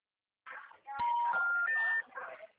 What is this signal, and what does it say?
Special information tone, with noise.